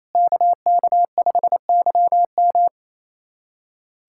Morse code, heard as a keyed tone, about 27 wpm, 700 Hz. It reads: KK5YM
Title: USA Callsign DE Callsign - 5 Characters - 28wpm